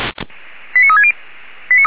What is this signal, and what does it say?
Five-tone selective call (ZVEI), heard as quick runs of five tones, tones of about 70 ms. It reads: 89360